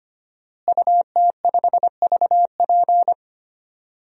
Morse code, heard as a keyed tone, about 25 wpm, 700 Hz. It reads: UT5VP
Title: DXCC-100 Call Signs - 5 Characters - Repeated 3x - 25wpm